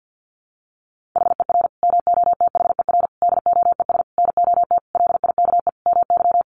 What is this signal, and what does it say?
Morse code, heard as a keyed tone, about 50 wpm, 700 Hz. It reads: HER MOTHER DOES NOT LIKE MY
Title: Sentences from Top 1400 Words 50wpm